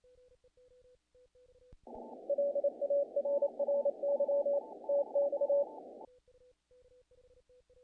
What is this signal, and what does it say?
Morse code, heard as a keyed tone, about 27 wpm, 570 Hz. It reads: LARRY TX
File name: contests / naqp-cw-k5tr-op-k5pi- -1800